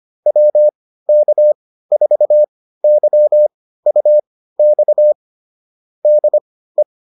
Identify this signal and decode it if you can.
Morse code, heard as a keyed tone, about 25 wpm, spaced as for 15 wpm, 600 Hz. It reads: WK4YUX DE